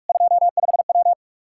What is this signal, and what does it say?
Morse code, heard as a keyed tone, about 46 wpm, 700 Hz. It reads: 25W
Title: Practice QSO - Exchange 2 - Info - 45wpm